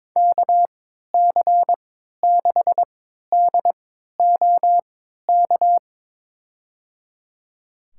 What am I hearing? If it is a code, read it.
Morse code, heard as a keyed tone, about 21 wpm, spaced as for 12 wpm, 700 Hz. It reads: KC6DOK
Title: USA Call Signs - 6 Characters - 3x Character Spacing - 22wpm